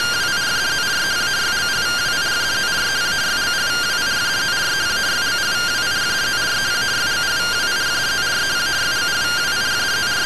Two-tone teleprinter signal, about 170 Hz shift.